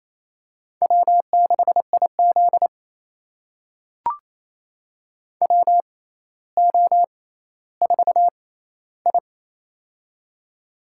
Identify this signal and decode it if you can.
Morse code, heard as a keyed tone, about 28 wpm, spaced as for 14 wpm, 700 Hz. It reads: W6IZ WO4I
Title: USA Call Signs - 4 Characters - 6x Character Spacing - 28wpm